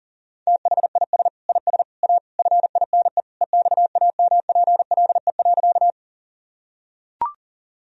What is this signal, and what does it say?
Morse code, heard as a keyed tone, about 40 wpm, 700 Hz. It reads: THIS IS A FINE EXAMPLE.
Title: Sentences from Top 1000 Words - Review Part 1 - 40wpm